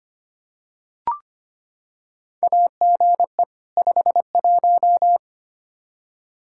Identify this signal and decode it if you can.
Morse code, heard as a keyed tone, about 25 wpm, 700 Hz. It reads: AGE 51
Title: Practice QSO - Exchange 3 - More Info - 25wpm